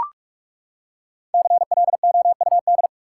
Morse code, heard as a keyed tone, about 43 wpm, 700 Hz. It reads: CLOUD
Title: Five Letter Words - 45wpm